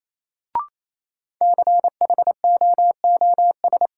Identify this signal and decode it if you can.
Morse code, heard as a keyed tone, about 28 wpm, 700 Hz. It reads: CHOOS
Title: Sentences from Top 700 Words - New Word Drill - 28wpm